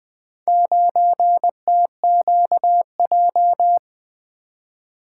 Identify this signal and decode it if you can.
Morse code, heard as a keyed tone, about 20 wpm, 700 Hz. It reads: 9TQJ
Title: Four Letter-Number Combinations - 20wpm